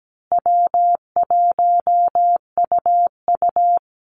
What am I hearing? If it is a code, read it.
Morse code, heard as a keyed tone, about 17 wpm, 700 Hz. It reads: W1UU
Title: Call Signs Repeated 2x - 17wpm